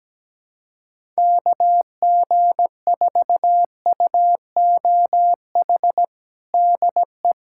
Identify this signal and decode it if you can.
Morse code, heard as a keyed tone, about 17 wpm, 700 Hz. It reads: KG4UOH DE
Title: USA Callsign DE Callsign - 6 Characters - 17wpm